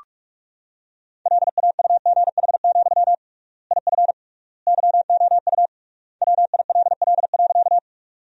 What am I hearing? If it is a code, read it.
Morse code, heard as a keyed tone, about 46 wpm, 700 Hz. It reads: LAUGH, IF YOU WILL.
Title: 3x Word-Spacing - Sentences from Top 500 Words - 45wpm